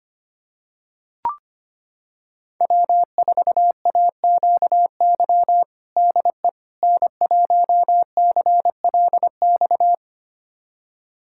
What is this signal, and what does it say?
Morse code, heard as a keyed tone, about 25 wpm, 700 Hz. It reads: W4AQY DE N1CLX
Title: USA Callsign DE Callsign - 5 Characters - 25wpm